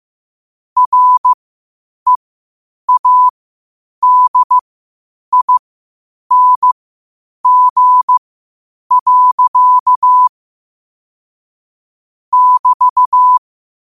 Morse code, heard as a keyed tone, about 15 wpm, spaced as for 8 wpm, 1000 Hz. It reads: READING. <BT>
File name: qotdfiles / 8wpm